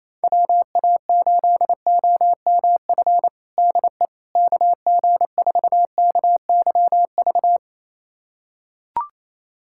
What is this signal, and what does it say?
Morse code, heard as a keyed tone, about 27 wpm, 700 Hz. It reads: WA8OMF DE KG4KYV
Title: USA Callsign DE Callsign - 6 Characters - 28wpm